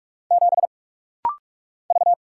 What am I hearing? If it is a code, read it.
Morse code, heard as a keyed tone, about 45 wpm, 700 Hz. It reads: 7 V
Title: Single Letter-Number - Rapid-Fire 45wpm